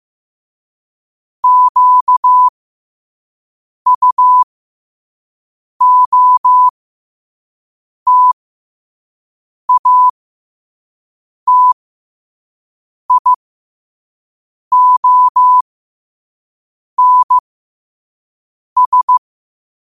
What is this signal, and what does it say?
Morse code, heard as a keyed tone, about 15 wpm, spaced as for 5 wpm, 1000 Hz. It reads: QUOTATIONS
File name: qotdfiles / 5wpm